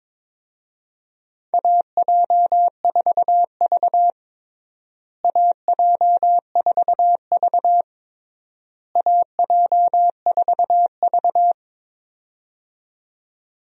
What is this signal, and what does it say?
Morse code, heard as a keyed tone, about 22 wpm, 700 Hz. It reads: AJ4V AJ4V AJ4V
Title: USA Call Signs - 4 Characters - Repeated 3x - 22wpm